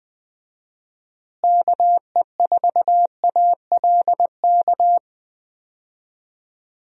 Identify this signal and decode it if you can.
Morse code, heard as a keyed tone, about 20 wpm, 700 Hz. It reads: KE4ALK
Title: USA Call Signs - 6 Characters - Repeated 3x - 20wpm